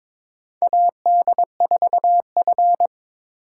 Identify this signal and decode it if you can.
Morse code, heard as a keyed tone, about 22 wpm, 700 Hz. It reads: AD4F
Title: USA Call Signs - 2x Character Spacing - 22wpm